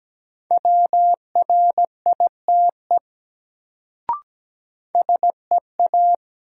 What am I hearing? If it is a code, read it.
Morse code, heard as a keyed tone, about 17 wpm, 700 Hz. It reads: WRITE SEA
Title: Sentences from Top 300 Words - New Word Drill - 17wpm